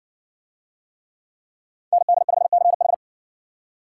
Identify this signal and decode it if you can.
Morse code, heard as a keyed tone, about 60 wpm, 700 Hz. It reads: ND5CH